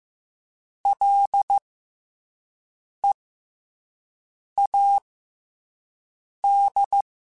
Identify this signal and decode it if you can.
Morse code, heard as a keyed tone, about 15 wpm, spaced as for 5 wpm, 800 Hz. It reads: LEAD
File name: F and L / F and L mp3 / F and L Prefixes 3x2 Twice @5